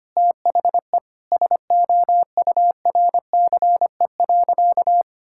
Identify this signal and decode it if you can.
Morse code, heard as a keyed tone, about 25 wpm, 700 Hz. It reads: THE SOURCE.
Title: Sentences from Top 1000 Words - Review Part 5 - 25wpm